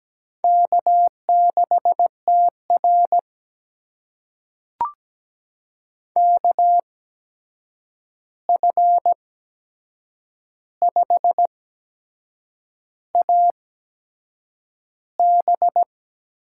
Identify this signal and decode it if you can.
Morse code, heard as a keyed tone, about 17 wpm, spaced as for 7 wpm, 700 Hz. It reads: K6TR KF5AB
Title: USA Call Signs - 8x Character Spacing - 17wpm